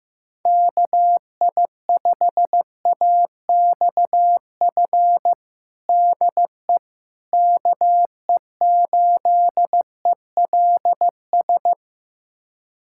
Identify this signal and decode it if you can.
Morse code, heard as a keyed tone, about 15 wpm, 700 Hz. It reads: KI5AXF DE KE8ELS
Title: USA Callsign DE Callsign - 6 Characters - 15wpm